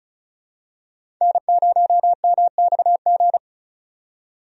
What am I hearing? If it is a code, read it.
Morse code, heard as a keyed tone, about 36 wpm, 700 Hz. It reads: N0MXG
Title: USA Call Signs - 5 Characters - 35wpm